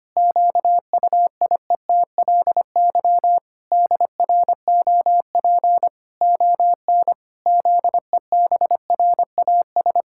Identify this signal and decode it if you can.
Morse code, heard as a keyed tone, about 25 wpm, 700 Hz. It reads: QUIETLY DROP ON ZEBRAS
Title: Pangram Challenge v2 - 25wpm